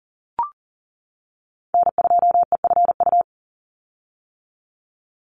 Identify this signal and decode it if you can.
Morse code, heard as a keyed tone, about 40 wpm, 690 Hz. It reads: N2EFU
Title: USA Call Signs 40wpm